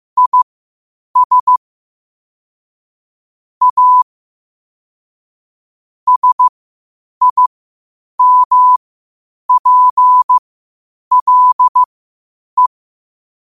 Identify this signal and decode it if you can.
Morse code, heard as a keyed tone, about 15 wpm, spaced as for 8 wpm, 1000 Hz. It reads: IS A SIMPLE
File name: qotdfiles / 8wpm